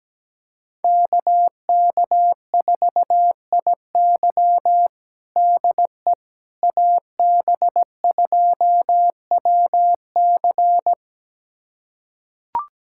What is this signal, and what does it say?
Morse code, heard as a keyed tone, about 17 wpm, 700 Hz. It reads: KK4IY DE AB2WC
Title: USA Callsign DE Callsign - 5 Characters - 17wpm